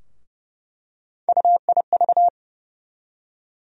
Morse code, heard as a keyed tone, about 30 wpm, 700 Hz. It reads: UIV
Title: USA Call Sign Postfixes - 3 Letters 30wpm